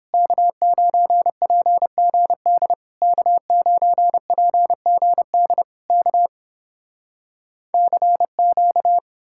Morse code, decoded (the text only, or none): K9PGD K9PGD K CQ